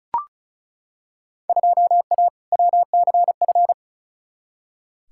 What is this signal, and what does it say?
Morse code, heard as a keyed tone, about 34 wpm, 700 Hz. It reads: 2A WCF
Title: Field Day Exchanges - 35wpm